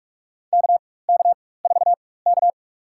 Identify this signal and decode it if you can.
Morse code, heard as a keyed tone, about 46 wpm, spaced as for 21 wpm, 700 Hz. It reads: KK4K
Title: USA Call Signs - 4 Characters - 4x Character Spacing - 45wpm